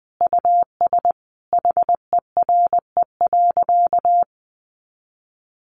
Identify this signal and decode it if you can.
Morse code, heard as a keyed tone, about 19 wpm, 700 Hz. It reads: US HERE.
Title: A Sound Of Thunder 20wpm